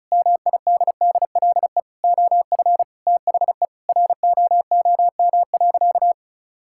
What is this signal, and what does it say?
Morse code, heard as a keyed tone, about 34 wpm, 700 Hz. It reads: MIDDLE OF THE ROOM.
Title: Sentences from Top 1000 Words - Review Part 6 - 35wpm